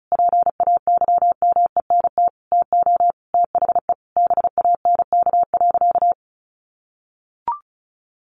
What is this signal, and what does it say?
Morse code, heard as a keyed tone, about 34 wpm, 700 Hz. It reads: PAYMENT TO THE BANK.
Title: Sentences from Top 1500 Words 35wpm